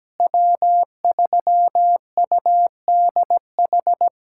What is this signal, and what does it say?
Morse code, heard as a keyed tone, about 17 wpm, 700 Hz. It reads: W3UDH
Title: USA Callsign DE Callsign - 5 Characters - 17wpm